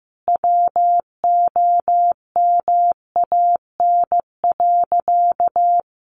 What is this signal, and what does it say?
Morse code, heard as a keyed tone, about 15 wpm, 700 Hz. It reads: WOMAN.
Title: Sentences from Top 1400 Words 15wpm